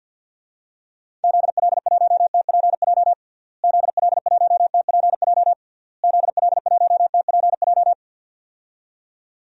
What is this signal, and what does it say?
Morse code, heard as a keyed tone, about 52 wpm, 700 Hz. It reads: ZL1TPJ ZL1TPJ ZL1TPJ
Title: DXCC-100 Call Signs - 6 Characters - Repeated 3x - 50wpm